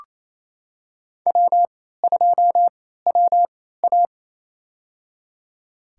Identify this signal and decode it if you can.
Morse code, heard as a keyed tone, about 27 wpm, spaced as for 16 wpm, 700 Hz. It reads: W2WA
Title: USA Call Signs - 4 Characters - 3x Character Spacing - 28wpm